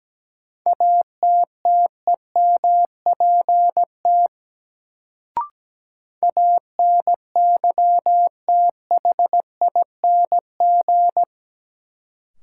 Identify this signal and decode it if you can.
Morse code, heard as a keyed tone, about 17 wpm, 700 Hz. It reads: ATTEMPT ANYTHING
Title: Sentences from Top 1000 Words - New Word Drill - 17wpm